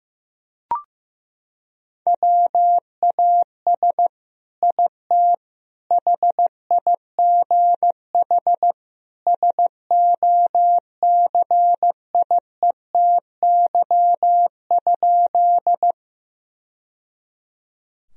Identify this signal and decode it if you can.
Morse code, heard as a keyed tone, about 16 wpm, 700 Hz. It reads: WAS IT HIGH SOCIETY?
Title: Sentences from Top 1000 Words - Review Part 4 - 15wpm